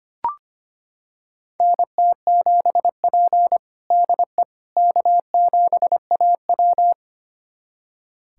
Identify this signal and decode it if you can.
Morse code, heard as a keyed tone, about 25 wpm, 700 Hz. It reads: NT7P DE K7AW
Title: USA Callsign DE Callsign - 4 Characters - 25wpm